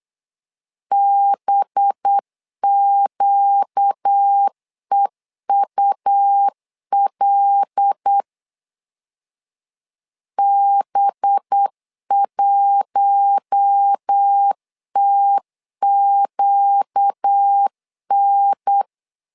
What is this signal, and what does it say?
Morse code, heard as a keyed tone, about 8 wpm, 800 Hz. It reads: BQEUL B1TQN
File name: LEZIONI / 5 LBVC / 5 LB 8wpm 2816